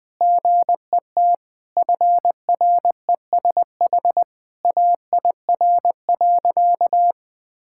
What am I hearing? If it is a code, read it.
Morse code, heard as a keyed tone, about 20 wpm, 700 Hz. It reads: GET FRESH AIR.
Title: Sentences from Top 1600 Words 20wpm